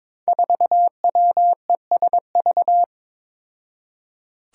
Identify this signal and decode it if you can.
Morse code, heard as a keyed tone, about 22 wpm, 700 Hz. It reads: 4WESV
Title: Five Challenging Letter-Number Combinations 22wpm